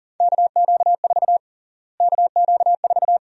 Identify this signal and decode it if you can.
Morse code, heard as a keyed tone, about 40 wpm, 700 Hz. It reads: KQ4 KQ4
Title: USA Call Sign Prefixes - Repeated 2x - 40wpm